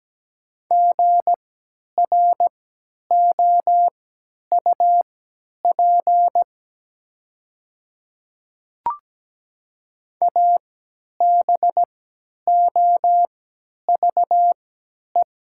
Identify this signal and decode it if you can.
Morse code, heard as a keyed tone, about 17 wpm, spaced as for 10 wpm, 700 Hz. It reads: GROUP ABOVE
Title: 3x Character Spacing - Top 500 Words - 17wpm